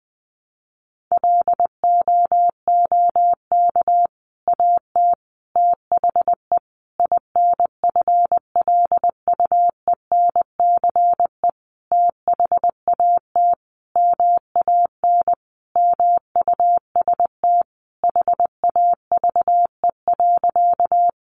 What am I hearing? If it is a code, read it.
Morse code, heard as a keyed tone, about 20 wpm, 700 Hz. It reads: LOOK AT THE INFLUENCE THAT MAN MUST HAVE.
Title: Sentences from Top 1100 Words 20wpm